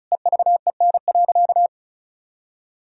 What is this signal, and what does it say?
Morse code, heard as a keyed tone, about 36 wpm, 700 Hz. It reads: EVEN.
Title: Sentences from Top 1000 Words - Review Part 1 - 35wpm